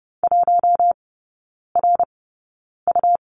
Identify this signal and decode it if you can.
Morse code, heard as a keyed tone, about 30 wpm, spaced as for 9 wpm, 700 Hz. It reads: 1RU
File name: Lesson 17e - Letter u - Three Characters 30wpm